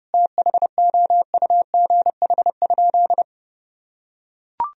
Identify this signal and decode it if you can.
Morse code, heard as a keyed tone, about 30 wpm, 700 Hz. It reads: THOUGH?
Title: Sentences from Top 1000 Words - Review Part 2 - 30wpm